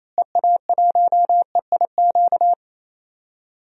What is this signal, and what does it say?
Morse code, heard as a keyed tone, about 28 wpm, 700 Hz. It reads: EA1EIQ
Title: DXCC-100 Call Signs - 6 Characters - Repeated 3x - 28wpm